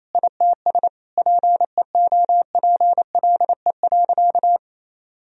Morse code, heard as a keyed tone, about 28 wpm, 700 Hz. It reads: ITS PEOPLE.
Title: Sentences from Top 1000 Words - Review Part 3 - 28wpm